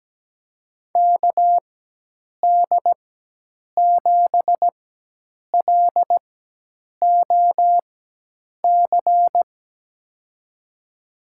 Morse code, heard as a keyed tone, about 17 wpm, spaced as for 8 wpm, 700 Hz. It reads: KD7LOC
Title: USA Call Signs - 6 Characters - 4x Character Spacing - 17wpm